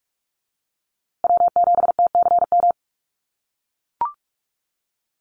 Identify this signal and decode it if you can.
Morse code, heard as a keyed tone, about 44 wpm, 700 Hz. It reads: W7TCM
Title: USA Call Signs Repeated 2x 45wpm